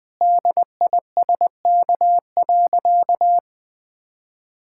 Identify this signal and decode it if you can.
Morse code, heard as a keyed tone, about 20 wpm, 700 Hz. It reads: DISK.
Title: Sentences from Top 1000 Words - Review Part 6 - 20wpm